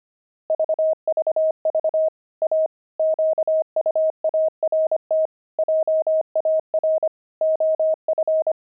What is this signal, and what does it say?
Morse code, heard as a keyed tone, about 25 wpm, 630 Hz. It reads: VVV A QUART JAR OF
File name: pangram-fs-25-cs-25-oxide